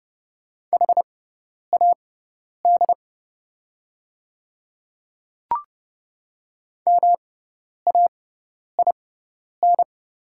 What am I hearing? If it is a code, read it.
Morse code, heard as a keyed tone, about 30 wpm, spaced as for 10 wpm, 700 Hz. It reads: HAD MAIN
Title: 6x Character Spacing - Top 300 Words - 30wpm